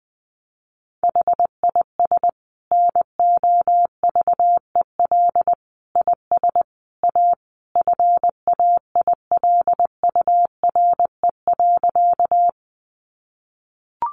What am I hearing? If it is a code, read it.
Morse code, heard as a keyed tone, about 20 wpm, 700 Hz. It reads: HIS NOVEL IS A FAILURE.
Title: Sentences from Top 1300 Words 20wpm